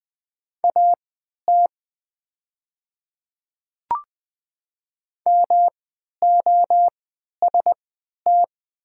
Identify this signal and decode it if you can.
Morse code, heard as a keyed tone, about 20 wpm, spaced as for 11 wpm, 700 Hz. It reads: AT MOST
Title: 3x Character Spacing - Top 200 Words - 20wpm